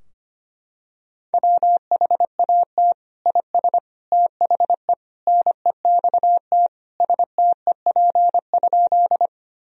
Morse code, heard as a keyed tone, about 25 wpm, 700 Hz. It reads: WHAT IS THE NEXT STEP?
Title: Sentences from Top 1000 Words - Review Part 2 - 25wpm